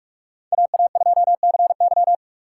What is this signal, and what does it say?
Morse code, heard as a keyed tone, about 58 wpm, 700 Hz. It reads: AA2CY